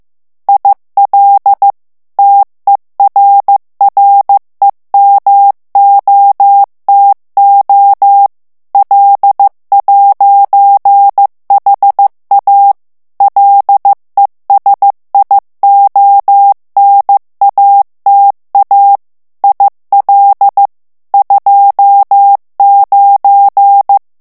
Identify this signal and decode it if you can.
Morse code, heard as a keyed tone, about 15 wpm, 800 Hz. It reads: IL TERREMOTO L'HA LESIONATA IL 29
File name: morse mp3 / ITALIEN texte clair 1 vitesse 14 mots minute